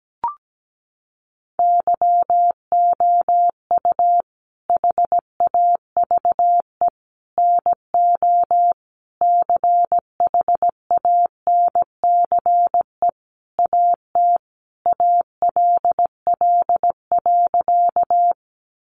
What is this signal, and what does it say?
Morse code, heard as a keyed tone, about 17 wpm, 700 Hz. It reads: YOU HAVE NO CHANCE AT ALL.
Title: Sentences from Top 800 Words 17wpm